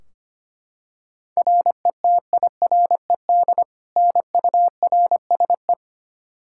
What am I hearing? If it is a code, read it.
Morse code, heard as a keyed tone, about 25 wpm, 700 Hz. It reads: RETIRED NURSE
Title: Practice QSO - Exchange 3 - More Info - 25wpm